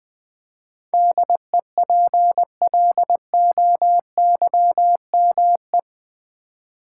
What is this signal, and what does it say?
Morse code, heard as a keyed tone, about 20 wpm, 700 Hz. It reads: DEPLOYME
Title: Build A Word - 4 Letters At A Time - 10-Letter Words - 20wpm